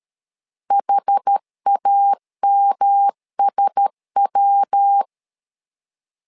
Morse code, heard as a keyed tone, about 13 wpm, 800 Hz. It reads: HAMSW
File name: LEZIONI / 4 Q1S3 12wpm 2023